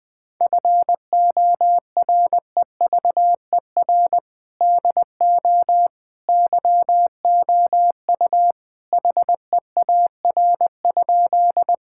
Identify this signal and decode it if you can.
Morse code, heard as a keyed tone, about 20 wpm, 700 Hz. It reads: FOREVER DO YOU HEAR?
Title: Sentences from Top 1600 Words 20wpm